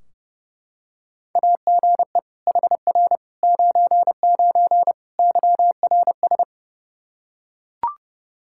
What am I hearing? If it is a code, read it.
Morse code, heard as a keyed tone, about 30 wpm, 700 Hz. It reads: AGE HR 99 YRS
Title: Practice QSO - Exchange 3 - More Info - 30wpm